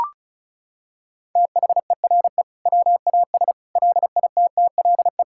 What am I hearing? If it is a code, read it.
Morse code, heard as a keyed tone, about 34 wpm, 700 Hz. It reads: THERE WAS LITTLE
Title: Sentences from Top 1000 Words - Review Part 6 - 35wpm